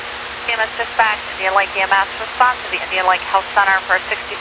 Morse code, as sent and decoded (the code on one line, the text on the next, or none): none